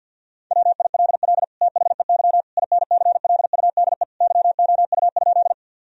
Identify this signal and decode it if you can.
Morse code, heard as a keyed tone, about 48 wpm, 700 Hz. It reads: WILL THEY INCLUDE YOU?
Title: Sentences from Top 1000 Words - Review Part 3 - 50wpm